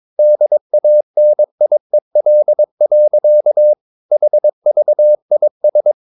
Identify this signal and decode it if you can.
Morse code, heard as a keyed tone, about 22 wpm, 600 Hz. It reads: DANIEL. HVIS
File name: Ferie QRS 22wpm